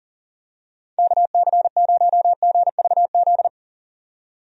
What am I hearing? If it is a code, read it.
Morse code, heard as a keyed tone, about 40 wpm, 700 Hz. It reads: KC0GVZ
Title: USA Call Signs - 6 Characters - 40wpm